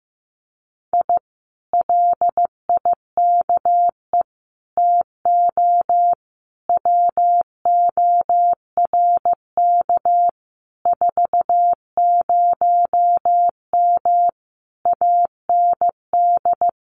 Morse code, heard as a keyed tone, about 15 wpm, 700 Hz. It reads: I LIKE TO WORK 40M AND
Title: Common QSO Phrases Encore 15wpm